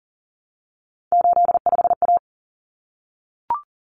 Morse code, heard as a keyed tone, about 40 wpm, 700 Hz. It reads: 85A
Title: Common Cut Numbers - 3-Digit 40wpm